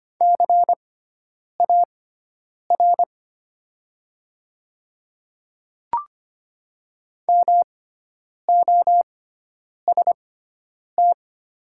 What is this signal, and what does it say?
Morse code, heard as a keyed tone, about 25 wpm, spaced as for 9 wpm, 700 Hz. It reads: CAR MOST
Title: 6x Character Spacing - Top 500 Words - 25wpm